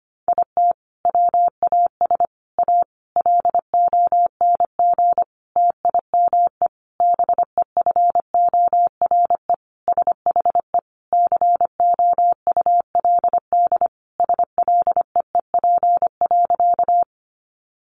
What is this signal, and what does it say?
Morse code, heard as a keyed tone, about 25 wpm, 700 Hz. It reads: IT WAS A LONG TIME BEFORE SHE COULD SLEEP.
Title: Sentences from Top 500 Words 25wpm